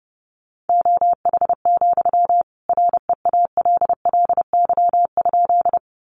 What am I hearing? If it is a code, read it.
Morse code, heard as a keyed tone, about 30 wpm, 700 Hz. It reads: OH, REALLY?
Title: Sentences from Top 600 Words 30wpm